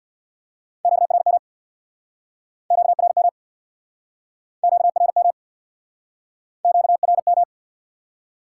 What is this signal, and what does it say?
Morse code, heard as a keyed tone, about 69 wpm, 700 Hz. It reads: QRM QRM QRM QRM